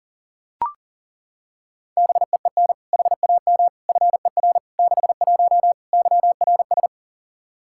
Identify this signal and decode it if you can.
Morse code, heard as a keyed tone, about 40 wpm, 700 Hz. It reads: BEEN HAM FER 61 YRS
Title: Practice QSO - Exchange 3 - More Info - 40wpm